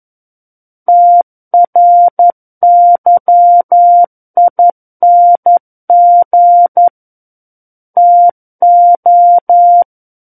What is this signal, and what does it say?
Morse code, heard as a keyed tone, about 11 wpm, 700 Hz. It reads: TRYING TO
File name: 11 WPM